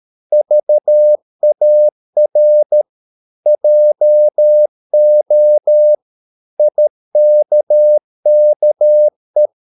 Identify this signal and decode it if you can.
Morse code, heard as a keyed tone, about 13 wpm, 600 Hz. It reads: VAR JO IKKE
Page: Ferie 13 wpm